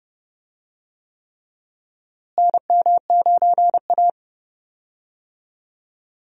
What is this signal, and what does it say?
Morse code, heard as a keyed tone, about 30 wpm, 700 Hz. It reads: NM9A
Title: USA Call Signs - 4 Characters - Repeated 3x - 30wpm